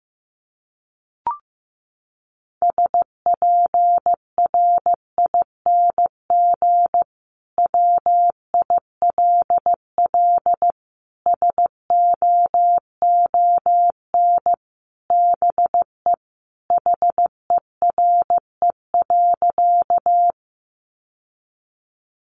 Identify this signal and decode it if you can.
Morse code, heard as a keyed tone, about 15 wpm, 700 Hz. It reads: SPRING WILL SOON BE HERE.
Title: Sentences from Top 1200 Words 15wpm